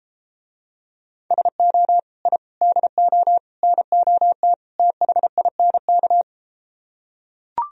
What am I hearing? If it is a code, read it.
Morse code, heard as a keyed tone, about 33 wpm, 700 Hz. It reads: SO I DO NOT THINK